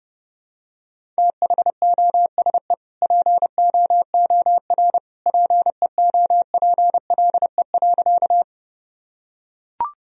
Morse code, heard as a keyed tone, about 30 wpm, 700 Hz. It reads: THOSE POOR PEOPLE.
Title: Sentences from Top 1000 Words - Review Part 1 - 30wpm